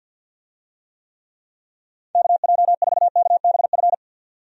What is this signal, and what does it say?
Morse code, heard as a keyed tone, about 51 wpm, 700 Hz. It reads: KJ4KBF
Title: USA Call Signs - 6 Characters - 50wpm